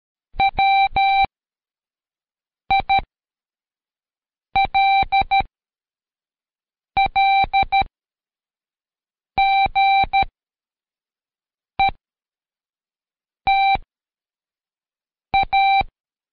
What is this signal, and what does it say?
Morse code, heard as a keyed tone, about 13 wpm, spaced as for 5 wpm, 780 Hz. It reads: WILL GET A
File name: Code Quick / AudioDownld2 / Seg102-Day12-FinalDrill